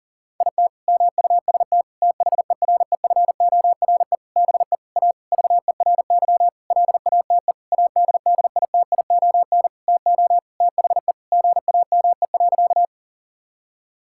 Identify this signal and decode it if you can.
Morse code, heard as a keyed tone, about 40 wpm, 700 Hz. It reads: IT MUST THEREFORE BE A VERY LATE ADDITION TO THE GAME.
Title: Sentences from Top 1000 Words - Review Part 6 - 40wpm